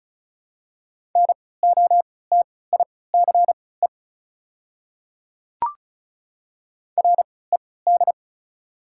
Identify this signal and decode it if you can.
Morse code, heard as a keyed tone, about 36 wpm, spaced as for 20 wpm, 690 Hz. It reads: NOTICE RED
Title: 3x Character Spacing - Top 500 Words - 35wpm